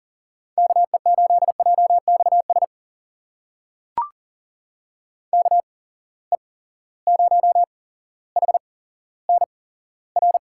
Morse code, heard as a keyed tone, about 40 wpm, spaced as for 17 wpm, 700 Hz. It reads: KE8JXS KE0HNR
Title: USA Call Signs - 6 Characters - 8x Character Spacing - 40wpm